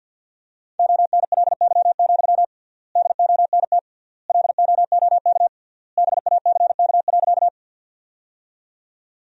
Morse code, heard as a keyed tone, about 50 wpm, 700 Hz. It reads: ONLY, DONT LOOK BACK.
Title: 3x Word-Spacing - Sentences from Top 300 Words - 50wpm